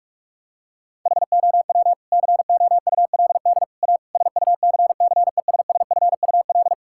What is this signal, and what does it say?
Morse code, heard as a keyed tone, about 46 wpm, 700 Hz. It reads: HOW COULD A SUCCESSFUL
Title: Sentences from Top 1000 Words - Review Part 6 - 45wpm